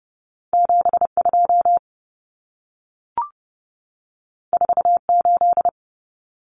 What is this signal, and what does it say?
Morse code, heard as a keyed tone, about 30 wpm, 700 Hz. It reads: 72 48
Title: Numbers - 2-Digit 30wpm